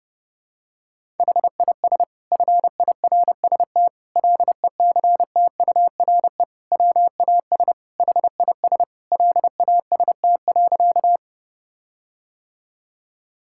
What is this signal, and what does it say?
Morse code, heard as a keyed tone, about 32 wpm, 700 Hz. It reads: HIS FIRST LECTURE WAS HIS LAST.
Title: Sentences from Top 1600 Words 30wpm